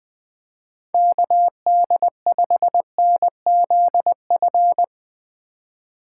Morse code, heard as a keyed tone, about 20 wpm, 700 Hz. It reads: KD5NZF
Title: USA Call Signs - 2x Character Spacing - 20wpm